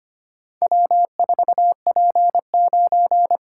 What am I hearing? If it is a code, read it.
Morse code, heard as a keyed tone, about 25 wpm, 700 Hz. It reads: W4P9
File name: practice / 25wpm / Four Letter-Number Combinations - 25wpm